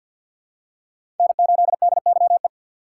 Morse code, heard as a keyed tone, about 48 wpm, 700 Hz. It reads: N8DYE
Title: USA Call Signs - 5 Characters - 4x Character Spacing - 50wpm